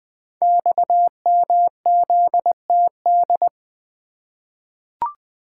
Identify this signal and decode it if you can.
Morse code, heard as a keyed tone, about 20 wpm, 700 Hz. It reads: XMZTD
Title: Code Groups - 5 Letter:Numbers - 20wpm